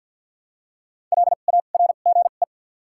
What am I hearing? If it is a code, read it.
Morse code, heard as a keyed tone, about 48 wpm, spaced as for 35 wpm, 700 Hz. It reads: LARGE
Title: 2x Character Spacing - Top 300 Words - 50wpm